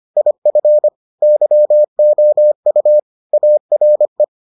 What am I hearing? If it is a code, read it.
Morse code, heard as a keyed tone, about 25 wpm, 600 Hz. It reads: IF YOU ARE